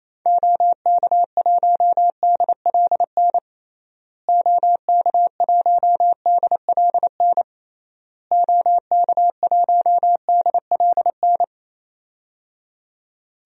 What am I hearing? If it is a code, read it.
Morse code, heard as a keyed tone, about 28 wpm, 700 Hz. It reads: OK1DLN OK1DLN OK1DLN
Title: DXCC-100 Call Signs - 6 Characters - Repeated 3x - 28wpm